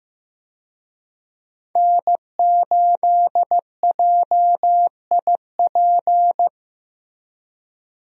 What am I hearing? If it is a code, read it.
Morse code, heard as a keyed tone, about 15 wpm, 700 Hz. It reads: N8JIP
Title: USA Call Signs - 5 Characters - Repeated 2x - 15wpm